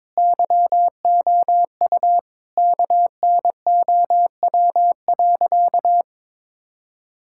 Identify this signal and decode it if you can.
Morse code, heard as a keyed tone, about 22 wpm, 700 Hz. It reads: YOU KNOW.
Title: Sentences from Top 1000 Words - Review Part 5 - 22wpm